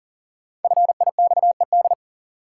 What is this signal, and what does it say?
Morse code, heard as a keyed tone, about 40 wpm, 700 Hz. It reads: FIXED
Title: Sentences from Top 1100 Words - New Word Drill Alt - 40wpm